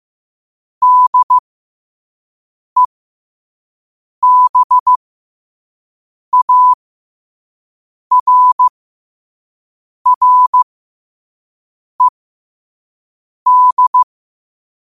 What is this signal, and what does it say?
Morse code, heard as a keyed tone, about 15 wpm, spaced as for 5 wpm, 1000 Hz. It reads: DEBARRED
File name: qotdfiles / 5wpm